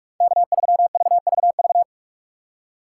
Morse code, heard as a keyed tone, about 43 wpm, 700 Hz. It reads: K3VVV
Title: USA Call Signs - 5 Characters - 2x Character Spacing - 45wpm